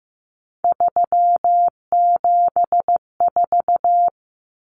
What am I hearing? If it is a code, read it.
Morse code, heard as a keyed tone, about 15 wpm, 700 Hz. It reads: 374
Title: Numbers - 3-Digit 15wpm